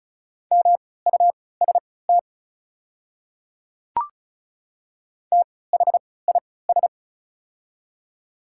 Morse code, heard as a keyed tone, about 36 wpm, spaced as for 20 wpm, 700 Hz. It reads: MUST THIS
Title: 3x Character Spacing - Top 200 Words - 35wpm